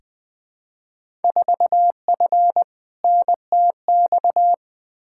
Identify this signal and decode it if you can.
Morse code, heard as a keyed tone, about 20 wpm, 700 Hz. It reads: 4F NTX
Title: Field Day Exchanges Encore - 20wpm